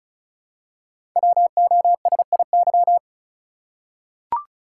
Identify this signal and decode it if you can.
Morse code, heard as a keyed tone, about 36 wpm, 700 Hz. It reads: WOSIY
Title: Code Groups - 5 Letter:Numbers - 35wpm